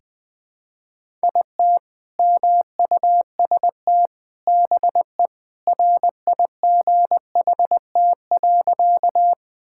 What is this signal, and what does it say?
Morse code, heard as a keyed tone, about 20 wpm, 700 Hz. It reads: IT MUST BE RIGHT.
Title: Sentences with 2-Word Phrases - Spoken Beginning - 20wpm